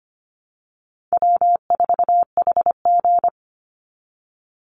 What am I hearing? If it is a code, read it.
Morse code, heard as a keyed tone, about 25 wpm, 700 Hz. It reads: W4HG
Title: Easy Call Signs Repeated 3x - 25wpm